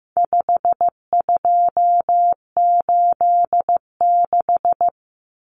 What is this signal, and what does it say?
Morse code, heard as a keyed tone, about 15 wpm, 700 Hz. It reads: 5286
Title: Numbers - 4-Digit 15wpm